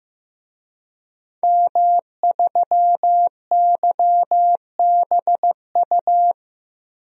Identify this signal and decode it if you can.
Morse code, heard as a keyed tone, about 15 wpm, 700 Hz. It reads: M3YBU
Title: DXCC-100 Call Signs - 5 Characters - 15wpm